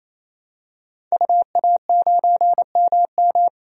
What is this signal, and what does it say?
Morse code, heard as a keyed tone, about 27 wpm, 700 Hz. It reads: UA9MM
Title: DXCC-100 Call Signs - 5 Characters - 28wpm